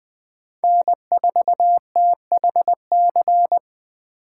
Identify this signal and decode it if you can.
Morse code, heard as a keyed tone, about 20 wpm, 700 Hz. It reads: N4THC
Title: USA Call Signs - 5 Characters - 8x Character Spacing - 20wpm